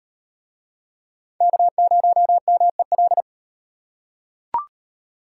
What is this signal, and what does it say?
Morse code, heard as a keyed tone, about 34 wpm, 700 Hz. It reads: K0MEL